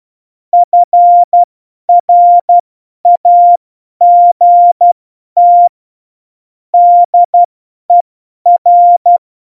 Morse code, heard as a keyed tone, about 12 wpm, spaced as for 10 wpm, 700 Hz. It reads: FRAGT DER
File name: Morsekurs / Witze / Witz 20. November 2018, 50 F70